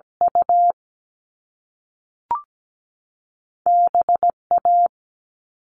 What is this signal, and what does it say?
Morse code, heard as a keyed tone, about 17 wpm, 700 Hz. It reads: U BA
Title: All Cut Numbers - 2-Digit 17wpm